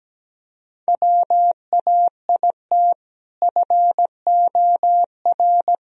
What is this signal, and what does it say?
Morse code, heard as a keyed tone, about 17 wpm, 700 Hz. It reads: WAIT FOR
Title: Sentences from Top 1000 Words - Review Part 4 - 17wpm